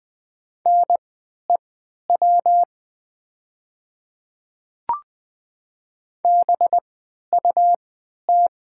Morse code, heard as a keyed tone, about 20 wpm, spaced as for 11 wpm, 700 Hz. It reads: NEW BUT
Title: 3x Character Spacing - Top 300 Words - 20wpm